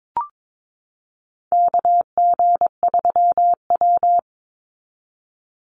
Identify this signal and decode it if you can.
Morse code, heard as a keyed tone, about 22 wpm, 700 Hz. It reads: KG3W
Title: NA SOTA Chaser Callsigns 22wpm